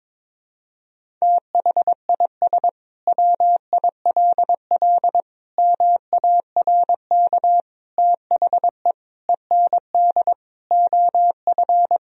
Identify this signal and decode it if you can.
Morse code, heard as a keyed tone, about 22 wpm, 700 Hz. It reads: THIS WILL MARK THE END OF
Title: Sentences from Top 1000 Words - Review Part 1 - 22wpm